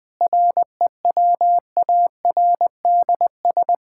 Morse code, heard as a keyed tone, about 20 wpm, 700 Hz. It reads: REWARDS
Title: Difficult Sentences with Hints - 20wpm